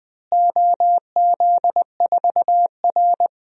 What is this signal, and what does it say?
Morse code, heard as a keyed tone, about 20 wpm, 700 Hz. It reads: OZ4R
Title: DXCC-100 Call Signs - 4 Characters - Repeated 2x - 20wpm